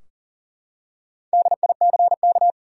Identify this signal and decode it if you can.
Morse code, heard as a keyed tone, about 38 wpm, 700 Hz. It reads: DICK
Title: USA Common Names - Max 5 Letters - 40wpm